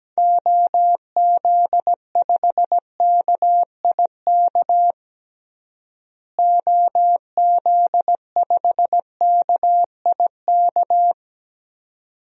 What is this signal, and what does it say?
Morse code, heard as a keyed tone, about 17 wpm, 700 Hz. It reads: OZ5KIK OZ5KIK
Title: DXCC-100 Call Signs - 6 Characters - Repeated 2x - 17wpm